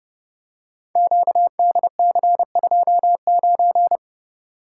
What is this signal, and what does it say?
Morse code, heard as a keyed tone, about 30 wpm, 700 Hz. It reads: QDC29
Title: Code Groups - 5 Letter:Numbers - 30wpm